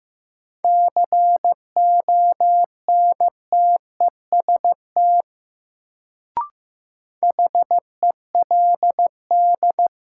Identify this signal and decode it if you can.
Morse code, heard as a keyed tone, about 15 wpm, 700 Hz. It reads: CONTEST HELD
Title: Sentences from Top 1600 Words - New Word Drill Alt - 15wpm